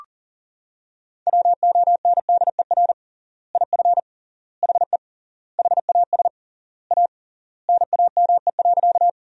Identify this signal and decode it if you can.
Morse code, heard as a keyed tone, about 40 wpm, 700 Hz. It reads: WONDER IF HE HAS A NAME.
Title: 3x Word-Spacing - Sentences from Top 500 Words - 40wpm